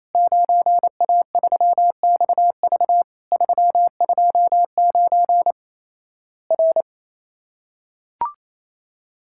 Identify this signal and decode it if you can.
Morse code, heard as a keyed tone, about 28 wpm, 700 Hz. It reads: 9A3XV 329 R
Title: Worked All Europe DX Contest - QTC Exchange - 28wpm